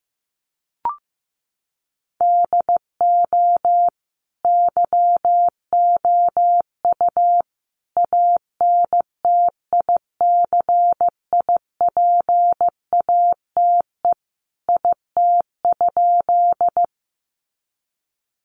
Morse code, heard as a keyed tone, about 15 wpm, 700 Hz. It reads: DO YOU ANTICIPATE IT?
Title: Sentences from Top 1500 Words 15wpm